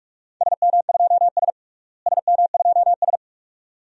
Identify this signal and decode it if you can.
Morse code, heard as a keyed tone, about 42 wpm, 700 Hz. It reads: SM2S SM2S
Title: DXCC-100 Call Signs - 4 Characters - Repeated 2x - 45wpm